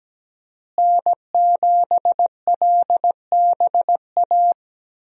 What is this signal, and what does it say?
Morse code, heard as a keyed tone, about 17 wpm, 700 Hz. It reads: N7LBA
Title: USA Call Signs - 5 Characters - Repeated 2x - 17wpm